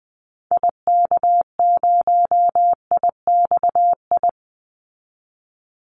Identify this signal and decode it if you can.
Morse code, heard as a keyed tone, about 20 wpm, 700 Hz. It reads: IK0IXI
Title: Call Signs Repeated 3x - 20wpm